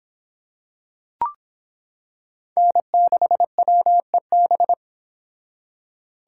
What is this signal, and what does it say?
Morse code, heard as a keyed tone, about 25 wpm, 700 Hz. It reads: N6WEB